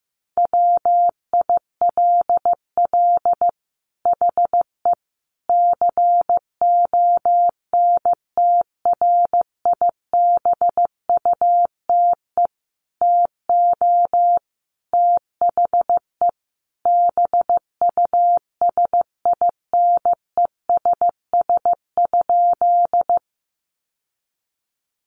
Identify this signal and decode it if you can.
Morse code, heard as a keyed tone, about 15 wpm, 700 Hz. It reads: WILL HE CONTRIBUTE TO THE BUSINESS?
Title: Sentences from Top 1400 Words 15wpm